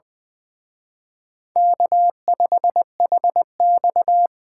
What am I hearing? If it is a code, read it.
Morse code, heard as a keyed tone, about 23 wpm, 700 Hz. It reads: K5HX
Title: Speed Racing - USA 1x2 Call Signs 20wpm